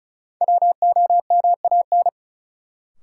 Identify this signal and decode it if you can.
Morse code, heard as a keyed tone, about 36 wpm, 700 Hz. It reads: WOMAN
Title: Sentences from Top 1000 Words - Review Part 2 - New Word Drill - 35wpm